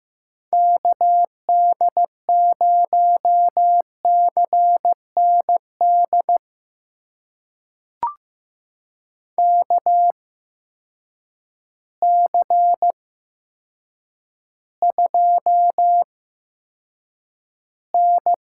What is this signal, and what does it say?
Morse code, heard as a keyed tone, about 15 wpm, spaced as for 8 wpm, 700 Hz. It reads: KD0CND KC2N
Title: USA Call Signs - 6 Characters - 8x Character Spacing - 15wpm